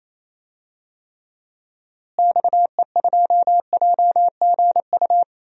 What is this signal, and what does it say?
Morse code, heard as a keyed tone, about 28 wpm, 700 Hz. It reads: XE2JGU
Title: DXCC-100 Call Signs - 6 Characters - Repeated 2x - 28wpm